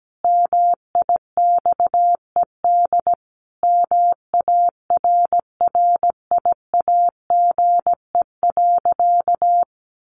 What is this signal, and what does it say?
Morse code, heard as a keyed tone, about 17 wpm, 700 Hz. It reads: MIXED MARRIAGE.
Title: Sentences from Top 1500 Words 17wpm